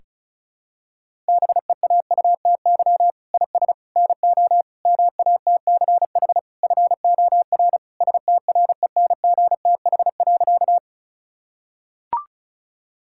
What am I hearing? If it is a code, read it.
Morse code, heard as a keyed tone, about 36 wpm, 700 Hz. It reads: BEAUTY IS NO MATCH FOR STRENGTH.
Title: Sentences from Top 1600 Words 35wpm